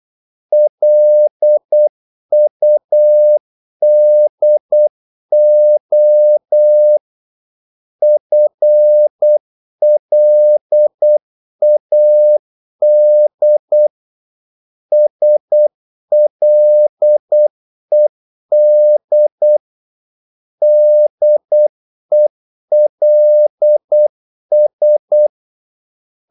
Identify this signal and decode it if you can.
Morse code, heard as a keyed tone, about 8 wpm, 600 Hz. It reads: LUDO FLAD SLED DELS
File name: Korte ord DK 8wpm